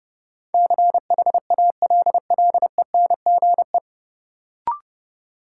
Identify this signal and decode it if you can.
Morse code, heard as a keyed tone, about 31 wpm, 700 Hz. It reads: CHALLENGE
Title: Sentences from Top 1000 Words - Review Part 7 - New Word Drill - 30wpm